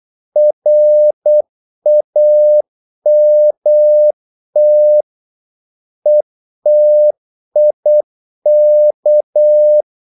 Morse code, heard as a keyed tone, about 8 wpm, 600 Hz. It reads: RAMT ETIK
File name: Korte ord DK 8wpm